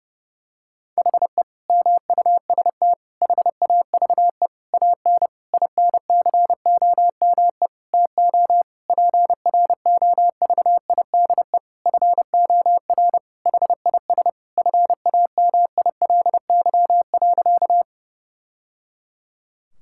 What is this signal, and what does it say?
Morse code, heard as a keyed tone, about 30 wpm, 700 Hz. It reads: HE MUST HAVE AN INCOME TO PROVIDE FOR HIS FAMILY.
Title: Sentences from Top 1000 Words - Review Part 5 - 30wpm